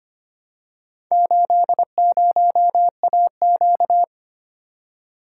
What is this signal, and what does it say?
Morse code, heard as a keyed tone, about 25 wpm, 700 Hz. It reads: 80AQ
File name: practice / 25wpm / Four Letter-Number Combinations - 25wpm